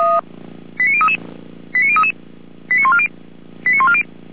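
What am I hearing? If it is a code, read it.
Five-tone selective call (ZVEI), heard as quick runs of five tones, tones of about 70 ms. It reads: 89033 89033 89140 89140
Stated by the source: Five-tone ZVEI selective call 89145, received by SDR.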